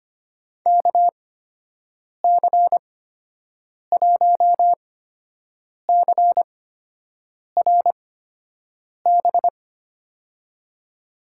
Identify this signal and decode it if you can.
Morse code, heard as a keyed tone, about 25 wpm, spaced as for 7 wpm, 700 Hz. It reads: KC1CRB
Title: USA Call Signs - 8x Character Spacing - 25wpm